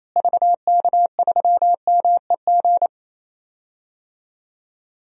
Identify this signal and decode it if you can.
Morse code, heard as a keyed tone, about 28 wpm, 700 Hz. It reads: VK3MEG
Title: DXCC-100 Call Signs - 6 Characters - Repeated 3x - 28wpm